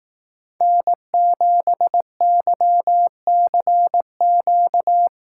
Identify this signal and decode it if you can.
Morse code, heard as a keyed tone, about 18 wpm, 700 Hz. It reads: N7YCQ
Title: Speed Racing - USA 1x3 Call Signs 15wpm